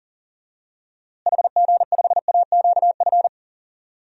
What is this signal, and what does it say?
Morse code, heard as a keyed tone, about 40 wpm, 700 Hz. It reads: HG5AQF
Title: DXCC-100 Call Signs - 6 Characters - Repeated 3x - 40wpm